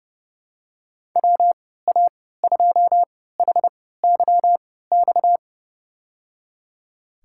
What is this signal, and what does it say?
Morse code, heard as a keyed tone, about 30 wpm, spaced as for 17 wpm, 700 Hz. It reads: WA2HYX
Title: USA Call Signs - 6 Characters - 3x Character Spacing - 30wpm